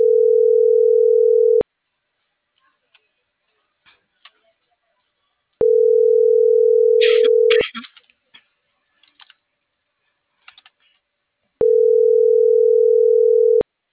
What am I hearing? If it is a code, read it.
Ringback tone (North America), 440 + 480 Hz.